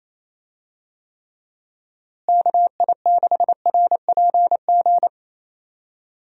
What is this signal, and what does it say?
Morse code, heard as a keyed tone, about 27 wpm, 700 Hz. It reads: KI6RPG
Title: USA Call Signs - 6 Characters - Repeated 2x - 28wpm